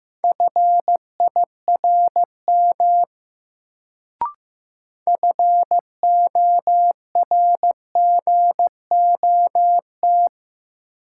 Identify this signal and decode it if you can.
Morse code, heard as a keyed tone, about 15 wpm, 700 Hz. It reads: FIRM FORGOT
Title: Sentences from Top 1400 Words - New Word Drill Alt - 15wpm